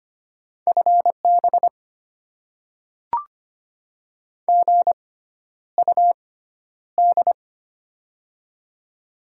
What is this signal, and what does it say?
Morse code, heard as a keyed tone, about 25 wpm, spaced as for 11 wpm, 700 Hz. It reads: FB GUD
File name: Top 20 QSO Elements - 6x Character Spacing - 25wpm